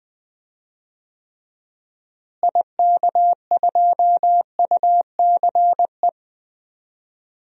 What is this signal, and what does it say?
Morse code, heard as a keyed tone, about 20 wpm, 700 Hz. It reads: IK2UCE
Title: DXCC-100 Call Signs - 6 Characters - Repeated 3x - 20wpm